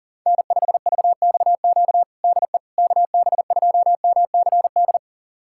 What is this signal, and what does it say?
Morse code, heard as a keyed tone, about 40 wpm, 700 Hz. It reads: N5VXQ DE KB2MCD
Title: USA Callsign DE Callsign 40wpm